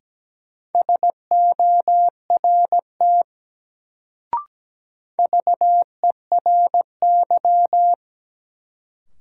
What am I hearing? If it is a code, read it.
Morse code, heard as a keyed tone, about 17 wpm, 700 Hz. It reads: SORT VERY
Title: Sets of 2 Words - New Word Drill - 17wpm